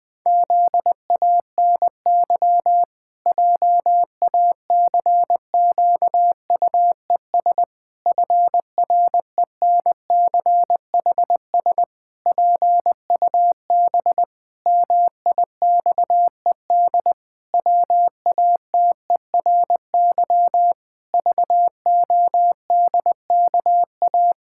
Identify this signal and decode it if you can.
Morse code, heard as a keyed tone, about 19 wpm, 700 Hz. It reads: ZANY JACQUES FRENCHS PUB MIXED WATERY VODKA
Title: Pangram Challenge v2 - 20wpm